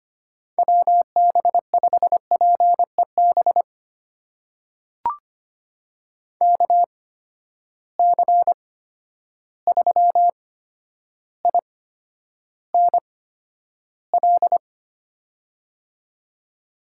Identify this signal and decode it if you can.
Morse code, heard as a keyed tone, about 25 wpm, spaced as for 11 wpm, 700 Hz. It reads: WB5PEB KC3INL
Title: USA Call Signs - 6 Characters - 8x Character Spacing - 25wpm